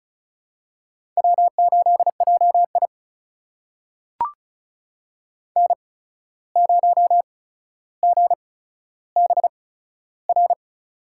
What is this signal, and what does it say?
Morse code, heard as a keyed tone, about 36 wpm, spaced as for 14 wpm, 700 Hz. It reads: W8JI N0GBR
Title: USA Call Signs - 8x Character Spacing - 35wpm